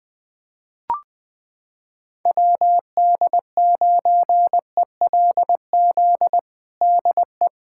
Morse code, heard as a keyed tone, about 21 wpm, 700 Hz. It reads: WD9ELZ DE
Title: USA Callsign DE Callsign - 6 Characters - 20wpm